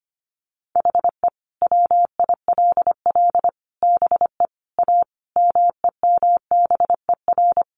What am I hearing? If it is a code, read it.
Morse code, heard as a keyed tone, about 25 wpm, 700 Hz. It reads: HE WILL BE A MEMBER
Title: Sentences from Top 1100 Words 25wpm